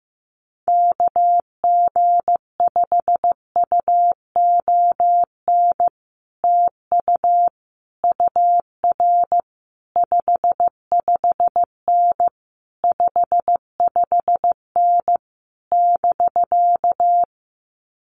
Morse code, heard as a keyed tone, about 15 wpm, 700 Hz. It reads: KG5UON TU UR 55N 55N <BK>
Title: Common QSO Phrases Encore 15wpm